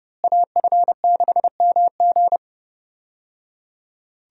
Morse code, heard as a keyed tone, about 30 wpm, 700 Hz. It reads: AF6MG
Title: USA Call Signs - 5 Characters - Repeated 3x - 30wpm